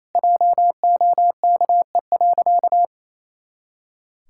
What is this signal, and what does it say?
Morse code, heard as a keyed tone, about 28 wpm, 700 Hz. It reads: JOKE.
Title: Pangram Challenge v2 - 28wpm